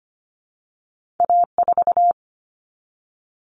Morse code, heard as a keyed tone, about 25 wpm, 700 Hz. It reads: A4
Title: Common Cut Numbers - 2-Digit 25wpm